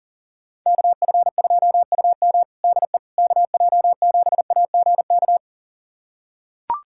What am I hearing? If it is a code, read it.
Morse code, heard as a keyed tone, about 40 wpm, 700 Hz. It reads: KF2UM DE KJ7AGK
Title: USA Callsign DE Callsign 40wpm